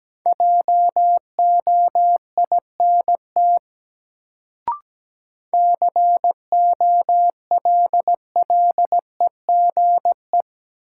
Morse code, heard as a keyed tone, about 17 wpm, 700 Hz. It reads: JOINT COLLEGE
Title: Sentences from Top 1300 Words - New Word Drill Alt - 17wpm